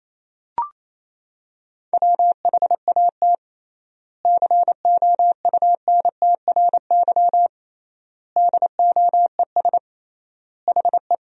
Morse code, heard as a keyed tone, about 28 wpm, 700 Hz. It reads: WHAT COUNTRY DOES HE
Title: 3x Word-Spacing - Sentences from Top 300 Words - 28wpm